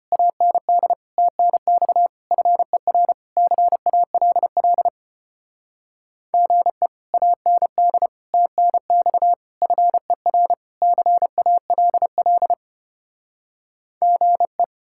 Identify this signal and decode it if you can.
Morse code, heard as a keyed tone, about 33 wpm, 700 Hz. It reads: AND TNX FER CALL GE AND TNX FER CALL GE